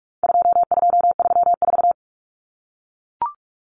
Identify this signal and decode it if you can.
Morse code, heard as a keyed tone, about 44 wpm, 700 Hz. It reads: 2234
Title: Common Cut Numbers - 4-Digit 45wpm